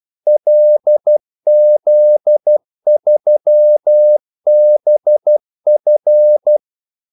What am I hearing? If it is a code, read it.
Morse code, heard as a keyed tone, about 12 wpm, 600 Hz. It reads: LZ3BF
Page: 2 ekstra ordmellemrum mellem kaldesignalerne